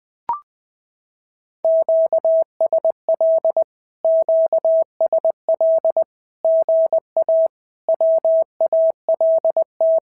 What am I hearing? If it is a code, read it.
Morse code, heard as a keyed tone, about 20 wpm, 650 Hz. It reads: QSL QSL GA WALT
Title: SOTA QSO Practice 20wpm